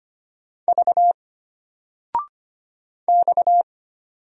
Morse code, heard as a keyed tone, about 25 wpm, 700 Hz. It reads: V X
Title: Single Letter-Number 25wpm